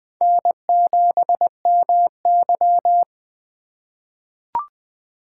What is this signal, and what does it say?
Morse code, heard as a keyed tone, about 20 wpm, 700 Hz. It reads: N7MY
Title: USA Call Signs - 8x Character Spacing - 20wpm